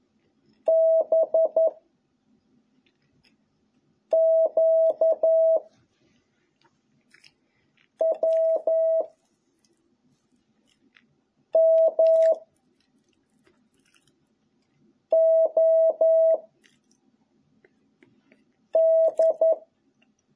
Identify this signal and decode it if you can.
Morse code, heard as a keyed tone, about 11 wpm, 660 Hz. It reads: B Q W M O D